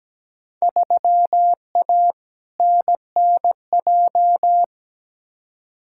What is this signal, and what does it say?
Morse code, heard as a keyed tone, about 17 wpm, 700 Hz. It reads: 3A NNJ
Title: Field Day Exchanges - 17wpm